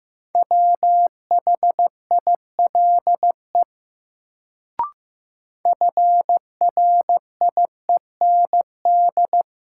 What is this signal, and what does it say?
Morse code, heard as a keyed tone, about 15 wpm, 700 Hz. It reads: WHILE FRIEND
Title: Sentences from Top 300 Words - New Word Drill - 15wpm